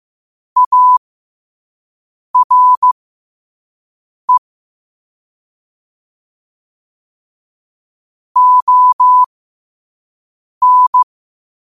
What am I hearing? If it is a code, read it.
Morse code, heard as a keyed tone, about 15 wpm, spaced as for 5 wpm, 1000 Hz. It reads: ARE ON
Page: New quotes every day in morse code at 5 Words per minute.